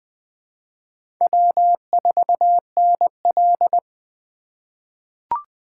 Morse code, heard as a keyed tone, about 20 wpm, 700 Hz. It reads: W4NL
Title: USA Call Signs - 4 Characters - 3x Character Spacing - 20wpm